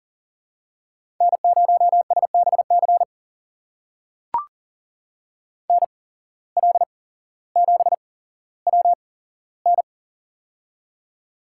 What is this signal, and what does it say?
Morse code, heard as a keyed tone, about 39 wpm, spaced as for 17 wpm, 700 Hz. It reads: N0SBC NL7WN
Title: USA Call Signs - 5 Characters - 8x Character Spacing - 40wpm